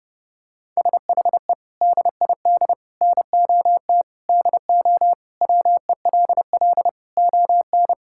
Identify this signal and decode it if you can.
Morse code, heard as a keyed tone, about 30 wpm, 700 Hz. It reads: SHE DID NOT DO WELL ON
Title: Sentences from Top 1000 Words - Review Part 5 - 30wpm